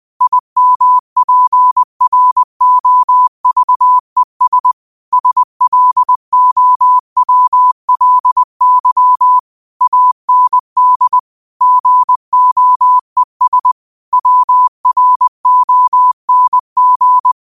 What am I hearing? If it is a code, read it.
Morse code, heard as a keyed tone, about 20 wpm, 1000 Hz. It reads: IMPROVES SLOWLY AND GOES WRONG
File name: qotdfiles / 20wpm